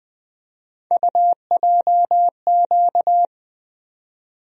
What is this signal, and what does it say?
Morse code, heard as a keyed tone, about 20 wpm, 700 Hz. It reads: UJQ
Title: USA Call Sign Postfixes - 3 Letters 20wpm